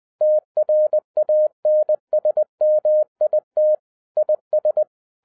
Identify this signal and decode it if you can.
Morse code, heard as a keyed tone, about 20 wpm, 610 Hz. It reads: TRANSMIT IS